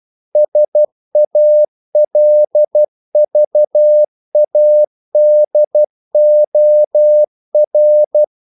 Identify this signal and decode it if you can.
Morse code, heard as a keyed tone, about 12 wpm, 600 Hz. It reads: SALVADOR
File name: CW / Practice / NationalCapitals12